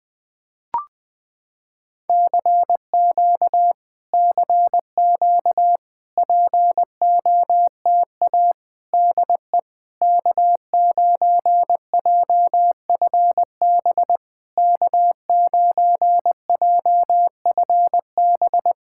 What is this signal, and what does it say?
Morse code, heard as a keyed tone, about 20 wpm, 700 Hz. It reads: CQ CQ POTA DE K9JFB K9JFB
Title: POTA QSO Practice - 20wpm